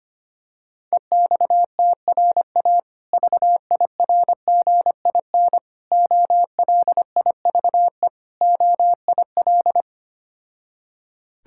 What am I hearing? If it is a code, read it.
Morse code, heard as a keyed tone, about 25 wpm, 700 Hz. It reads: EXTRA VIRGIN OLIVE OIL
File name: practice / 25wpm / Difficult Words with Hints - 25wpm